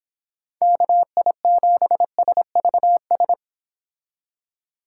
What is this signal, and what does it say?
Morse code, heard as a keyed tone, about 30 wpm, 700 Hz. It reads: KI7SVS